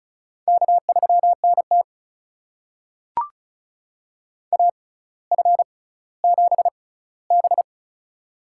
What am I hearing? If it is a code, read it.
Morse code, heard as a keyed tone, about 36 wpm, spaced as for 18 wpm, 700 Hz. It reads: K3NT AF7B
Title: USA Call Signs - 4 Characters - 6x Character Spacing - 35wpm